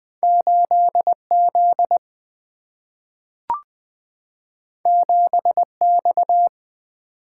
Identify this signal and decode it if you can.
Morse code, heard as a keyed tone, about 20 wpm, 700 Hz. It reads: 8 Z 7 X
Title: Single-Letter Single-Number 20wpm